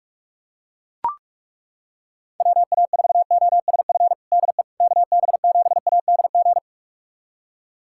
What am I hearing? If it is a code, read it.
Morse code, heard as a keyed tone, about 41 wpm, 700 Hz. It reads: WA4OSF DE KB7ADG
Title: USA Callsign DE Callsign - 6 Characters - 45wpm